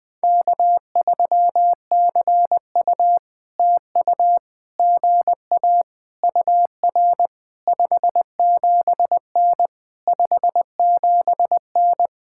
Morse code, heard as a keyed tone, about 20 wpm, 700 Hz. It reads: K3CU TU GA UR 57N 57N
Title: POTA QSO Practice - 20wpm